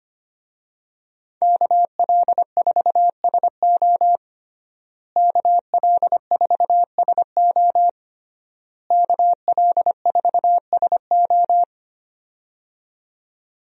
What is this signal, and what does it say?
Morse code, heard as a keyed tone, about 25 wpm, 700 Hz. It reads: KL4SO KL4SO KL4SO
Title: USA Call Signs - 5 Characters - Repeated 3x - 25wpm